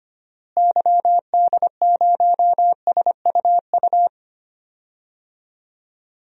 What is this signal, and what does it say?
Morse code, heard as a keyed tone, about 25 wpm, 700 Hz. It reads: YD0SUU
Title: DXCC-100 Call Signs - 6 Characters - Repeated 2x - 25wpm